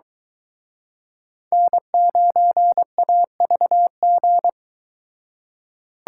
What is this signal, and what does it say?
Morse code, heard as a keyed tone, about 22 wpm, 700 Hz. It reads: N9AVG